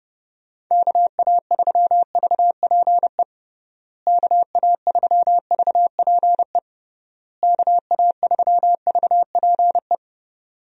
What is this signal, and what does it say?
Morse code, heard as a keyed tone, about 32 wpm, 700 Hz. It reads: KA3VPE KA3VPE KA3VPE
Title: USA Call Signs - 6 Characters - Repeated 3x - 30wpm